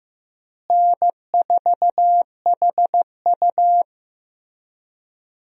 Morse code, heard as a keyed tone, about 15 wpm, 700 Hz. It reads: N4HU
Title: USA Call Signs - 4 Characters - Repeated 3x - 15wpm